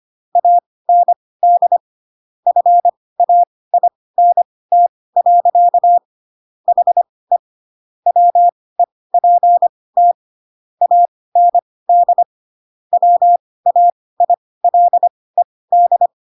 Morse code, heard as a keyed tone, about 25 wpm, spaced as for 18 wpm, 700 Hz. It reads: AND FAINT. HE WEPT AND WAILED